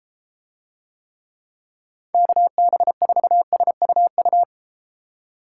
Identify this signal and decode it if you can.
Morse code, heard as a keyed tone, about 33 wpm, 700 Hz. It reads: KB4SUU